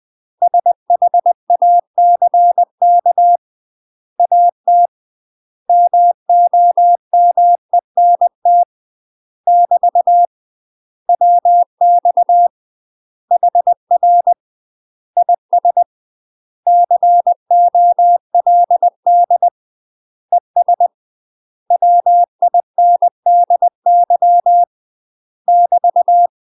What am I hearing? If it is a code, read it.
Morse code, heard as a keyed tone, about 20 wpm, 700 Hz. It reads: SHACK AT MOMENT <BT> WX HR IS COLD ES WINDY <BT>
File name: UK MASTER QSOs TRAINING / 20WPM / UK MASTER QSOs 0048